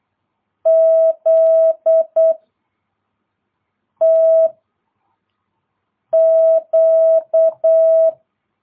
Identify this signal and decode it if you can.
Morse code, heard as a keyed tone, about 8 wpm, 660 Hz. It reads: Z T Q
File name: jh8cbh / 2022cw